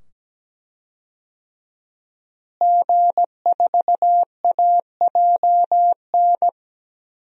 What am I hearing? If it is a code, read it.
Morse code, heard as a keyed tone, about 17 wpm, 700 Hz. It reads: G4AJN
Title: DXCC-100 Call Signs - 5 Characters - Repeated 2x - 17wpm